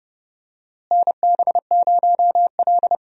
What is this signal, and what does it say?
Morse code, heard as a keyed tone, about 30 wpm, 700 Hz. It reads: NB0L
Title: USA Call Signs - 4 Characters - Repeated 2x - 30wpm